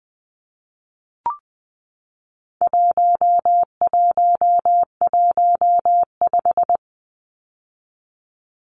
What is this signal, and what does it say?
Morse code, heard as a keyed tone, about 20 wpm, 700 Hz. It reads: 1115
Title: Numbers - 4-Digit 20wpm